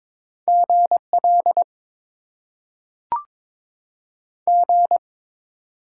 Morse code, heard as a keyed tone, about 22 wpm, 700 Hz. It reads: G L G
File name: Top 90 QSO Elements - 6x Character Spacing - 22wpm